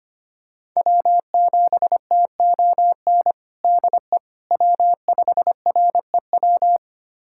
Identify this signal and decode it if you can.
Morse code, heard as a keyed tone, about 25 wpm, 700 Hz. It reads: W7TON DE W5REW
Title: USA Callsign DE Callsign - 5 Characters - 25wpm